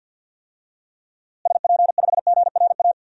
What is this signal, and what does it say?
Morse code, heard as a keyed tone, about 46 wpm, 700 Hz. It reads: SP5GRA
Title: DXCC-100 Call Signs - 6 Characters - Repeated 2x - 50wpm